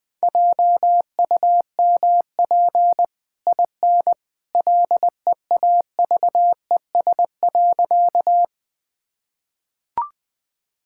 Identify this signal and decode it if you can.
Morse code, heard as a keyed tone, about 20 wpm, 700 Hz. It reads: JUMP IN LEAVES.
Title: Easy Sentences with Hints - 20wpm